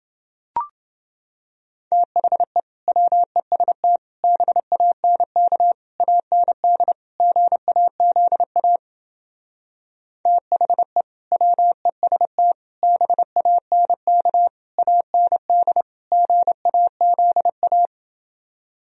Morse code, heard as a keyed tone, about 28 wpm, 700 Hz. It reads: THE WEST BANK AND GAZA THE WEST BANK AND GAZA